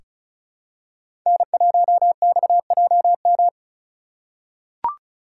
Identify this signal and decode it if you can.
Morse code, heard as a keyed tone, about 36 wpm, 700 Hz. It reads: N1XJM
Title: K1USN SST QSO Practice - 35wpm